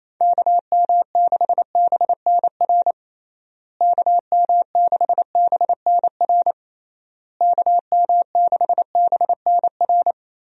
Morse code, heard as a keyed tone, about 28 wpm, 700 Hz. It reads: KM6BNR KM6BNR KM6BNR
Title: USA Call Signs - 6 Characters - Repeated 3x - 28wpm